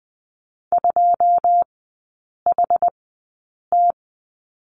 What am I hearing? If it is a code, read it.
Morse code, heard as a keyed tone, about 20 wpm, spaced as for 8 wpm, 700 Hz. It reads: 2HT
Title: 093 Lesson 16e - Letter c - Three Characters 20wpm